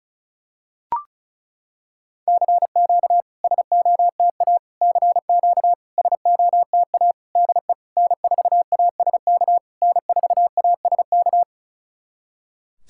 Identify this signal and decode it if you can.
Morse code, heard as a keyed tone, about 36 wpm, 690 Hz. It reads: CQ SOTA CQ SOTA DE N4ASK N4ASK
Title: SOTA QSO Practice 35wpm